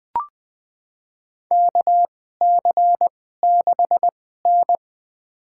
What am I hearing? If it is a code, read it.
Morse code, heard as a keyed tone, about 20 wpm, spaced as for 15 wpm, 700 Hz. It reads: KC6N
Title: USA Call Signs - 4 Characters - 2x Character Spacing - 20wpm